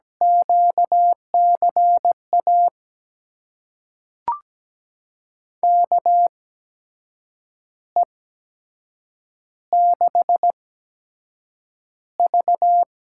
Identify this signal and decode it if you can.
Morse code, heard as a keyed tone, about 17 wpm, spaced as for 7 wpm, 700 Hz. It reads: QCA KE6V
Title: USA Call Signs - 6 Characters - 8x Character Spacing - 17wpm